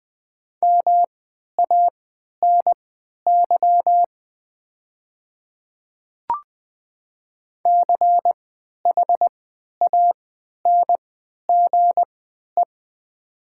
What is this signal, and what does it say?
Morse code, heard as a keyed tone, about 20 wpm, spaced as for 11 wpm, 700 Hz. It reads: MANY CHANGE
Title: 3x Character Spacing - Top 300 Words - 20wpm